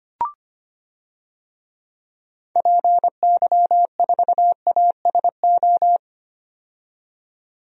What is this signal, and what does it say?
Morse code, heard as a keyed tone, about 25 wpm, 700 Hz. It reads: PY4ASO
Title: DXCC-100 Call Signs - 6 Characters - 25wpm